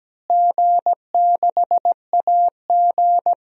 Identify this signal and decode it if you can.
Morse code, heard as a keyed tone, about 17 wpm, 700 Hz. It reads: G6AG
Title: DXCC-100 Call Signs - 4 Characters - Repeated 3x - 17wpm